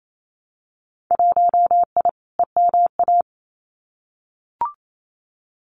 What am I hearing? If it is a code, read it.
Morse code, heard as a keyed tone, about 28 wpm, 700 Hz. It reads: 1I EMA
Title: Winter Field Day Exchanges Encore 28wpm